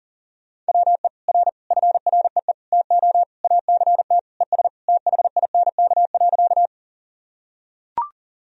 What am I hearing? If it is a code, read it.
Morse code, heard as a keyed tone, about 40 wpm, 700 Hz. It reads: WE R FREE TO ACT ES THINK.
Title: Sentences from Top 500 Words and Abbreviations 40wpm